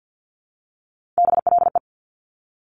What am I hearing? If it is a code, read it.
Morse code, heard as a keyed tone, about 50 wpm, 700 Hz. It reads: BLE
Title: Top Three Letter Combinations in Words 50wpm